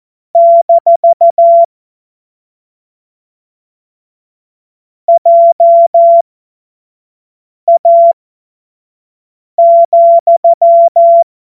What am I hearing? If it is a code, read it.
Morse code, heard as a keyed tone, about 14 wpm, spaced as for 5 wpm, 680 Hz. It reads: - JA,
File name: Morsekurs / Witze / Witz 30. Mai 2023 25 F70